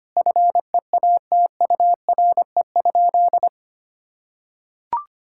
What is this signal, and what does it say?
Morse code, heard as a keyed tone, about 25 wpm, 700 Hz. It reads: FEATURE?
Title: Sentences from Top 1000 Words - Review Part 5 - 25wpm